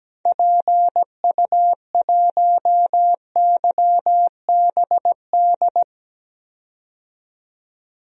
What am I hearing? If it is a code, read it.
Morse code, heard as a keyed tone, about 17 wpm, 700 Hz. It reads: PU1YBD
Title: DXCC-100 Call Signs - 6 Characters - 17wpm